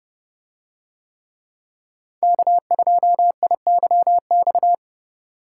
Morse code, heard as a keyed tone, about 29 wpm, 700 Hz. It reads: K2IYX